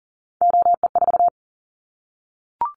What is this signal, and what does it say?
Morse code, heard as a keyed tone, about 40 wpm, 700 Hz. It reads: OE4
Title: Callsign Prefixes for Top 50 DXCC - 40wpm